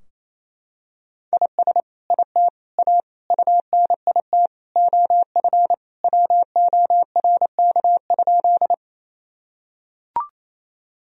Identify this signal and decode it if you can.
Morse code, heard as a keyed tone, about 28 wpm, 700 Hz. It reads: IS IT A UNIT OF WORK?
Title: Sentences from Top 1000 Words - Review Part 2 - 28wpm